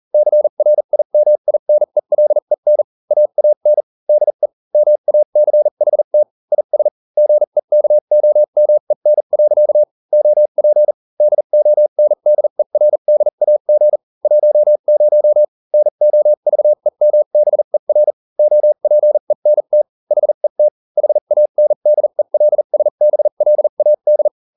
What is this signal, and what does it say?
Morse code, heard as a keyed tone, about 40 wpm, spaced as for 35 wpm, 600 Hz. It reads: CRIMINELEN AAN DE MACHT IS GEKOMEN. OP DONDERDAG 10 NOVEMBER OPENT HET HANDELSBLAD